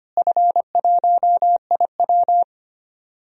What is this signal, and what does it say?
Morse code, heard as a keyed tone, about 25 wpm, 700 Hz. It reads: F1IW
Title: DXCC-100 Call Signs - 4 Characters - Repeated 3x - 25wpm